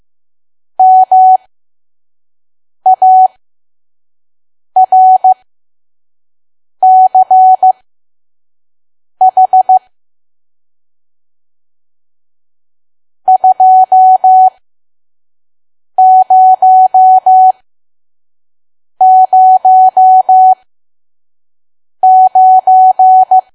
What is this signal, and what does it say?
Morse code, heard as a keyed tone, about 15 wpm, spaced as for 5 wpm, 750 Hz. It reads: MARCH 2009
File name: Archive / 5 WPM files / PM